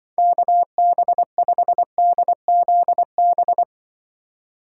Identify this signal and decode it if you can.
Morse code, heard as a keyed tone, about 23 wpm, 700 Hz. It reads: KB5DZB
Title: Speed Racing - USA 2x3 Call Signs 20wpm